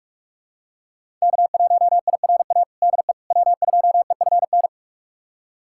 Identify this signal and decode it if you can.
Morse code, heard as a keyed tone, about 46 wpm, 700 Hz. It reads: K1IRA DE W2EFN
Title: USA Callsign DE Callsign - 5 Characters - 45wpm